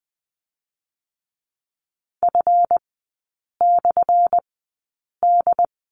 Morse code, heard as a keyed tone, about 20 wpm, spaced as for 8 wpm, 700 Hz. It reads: F/D
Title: 184 Lesson 30e - Letter b - Three Characters 20wpm